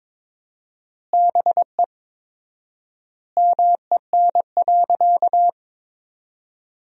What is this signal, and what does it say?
Morse code, heard as a keyed tone, about 22 wpm, 700 Hz. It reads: BE MEN.
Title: 4x Word-Spacing - Sentences from Top 300 Words - 22wpm